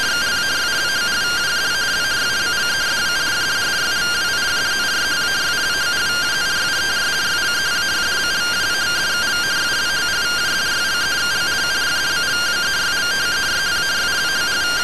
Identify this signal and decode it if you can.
Two-tone teleprinter signal, about 170 Hz shift.